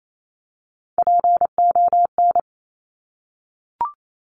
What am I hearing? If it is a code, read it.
Morse code, heard as a keyed tone, about 28 wpm, 700 Hz. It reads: PON
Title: Top Three Letter Combinations in Words 28wpm